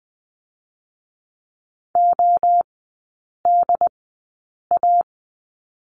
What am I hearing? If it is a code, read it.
Morse code, heard as a keyed tone, about 19 wpm, spaced as for 8 wpm, 700 Hz. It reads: ODA
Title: 067 Lesson 12e - Letter d - Three Characters 20wpm